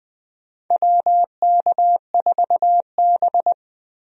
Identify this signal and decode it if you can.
Morse code, heard as a keyed tone, about 20 wpm, 700 Hz. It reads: WK4B
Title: USA Call Signs - 4 Characters - Repeated 3x - 20wpm